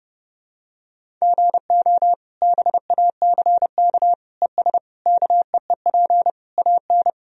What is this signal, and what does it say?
Morse code, heard as a keyed tone, about 30 wpm, 700 Hz. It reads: GO BACK ES KEEP AN
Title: Sentences from Top 300 Words and Abbreviations 30wpm